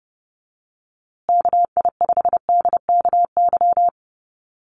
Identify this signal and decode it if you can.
Morse code, heard as a keyed tone, about 30 wpm, 700 Hz. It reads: KI5DKY
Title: USA Call Signs Repeated 2x 30wpm